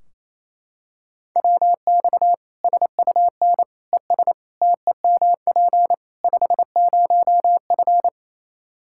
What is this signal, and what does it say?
Morse code, heard as a keyed tone, about 28 wpm, 700 Hz. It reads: WX SUN ES TEMP 50F
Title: Practice QSO - Exchange 2 - Info - 28wpm